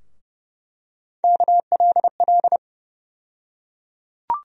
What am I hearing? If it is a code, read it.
Morse code, heard as a keyed tone, about 30 wpm, 700 Hz. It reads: KLL
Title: USA Call Sign Postfixes - 3 Letters 30wpm